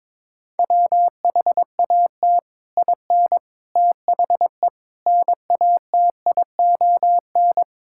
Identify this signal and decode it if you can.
Morse code, heard as a keyed tone, about 22 wpm, 700 Hz. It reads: WHAT IN THE NATION
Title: Sentences from Top 1000 Words - Review Part 6 - 22wpm